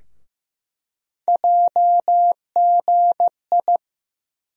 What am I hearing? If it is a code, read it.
Morse code, heard as a keyed tone, about 15 wpm, 700 Hz. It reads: JGI
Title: USA Call Sign Postfixes - 3 Letters 15wpm